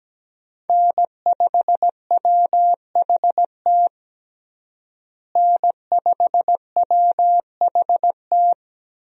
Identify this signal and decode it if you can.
Morse code, heard as a keyed tone, about 17 wpm, 700 Hz. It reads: N5WHT N5WHT
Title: USA Call Signs - 5 Characters - Repeated 2x - 17wpm